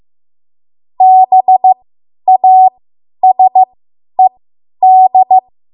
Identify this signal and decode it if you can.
Morse code, heard as a keyed tone, about 15 wpm, spaced as for 10 wpm, 750 Hz. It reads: BASED
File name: Morse / 10WPM